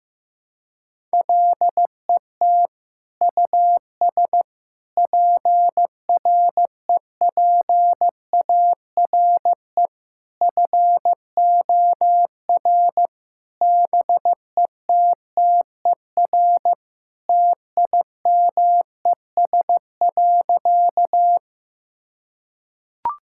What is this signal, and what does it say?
Morse code, heard as a keyed tone, about 16 wpm, 700 Hz. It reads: LET US PREPARE FOR BETTER TIMES.
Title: Sentences from Top 1000 Words - Review Part 6 - 15wpm